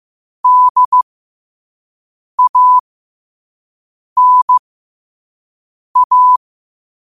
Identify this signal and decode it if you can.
Morse code, heard as a keyed tone, about 15 wpm, spaced as for 5 wpm, 1000 Hz. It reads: DANA